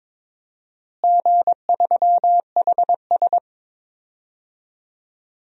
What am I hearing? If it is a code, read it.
Morse code, heard as a keyed tone, about 22 wpm, 700 Hz. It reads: G3HS
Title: DXCC-100 Call Signs - 4 Characters - Repeated 3x - 22wpm